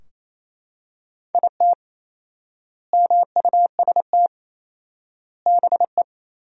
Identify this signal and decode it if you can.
Morse code, heard as a keyed tone, about 28 wpm, 700 Hz. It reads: IT MUST BE
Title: 4x Word-Spacing - Sentences from Top 200 Words - 28wpm